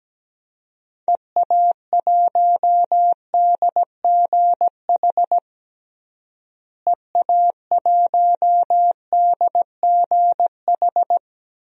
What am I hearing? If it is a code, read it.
Morse code, heard as a keyed tone, about 17 wpm, 700 Hz. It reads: EA1DGH EA1DGH
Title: DXCC-100 Call Signs - 6 Characters - Repeated 3x - 17wpm